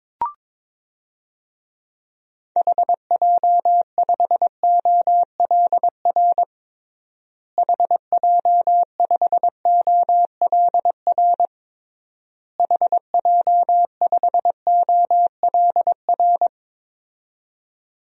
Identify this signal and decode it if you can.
Morse code, heard as a keyed tone, about 21 wpm, 700 Hz. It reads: HJ5OLR HJ5OLR HJ5OLR
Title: DXCC-100 Call Signs - 6 Characters - Repeated 3x - 22wpm